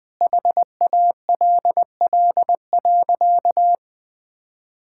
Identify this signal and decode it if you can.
Morse code, heard as a keyed tone, about 20 wpm, 700 Hz. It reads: HALL.
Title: Sentences with 8-Word Phrases - 20wpm